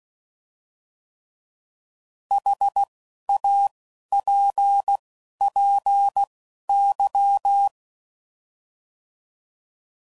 Morse code, heard as a keyed tone, about 16 wpm, spaced as for 11 wpm, 800 Hz. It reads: HAPPY
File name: Mixed / Misc / Words